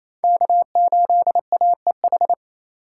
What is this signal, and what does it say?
Morse code, heard as a keyed tone, about 28 wpm, 700 Hz. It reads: K8AEH
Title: USA Call Signs - 5 Characters - 4x Character Spacing - 28wpm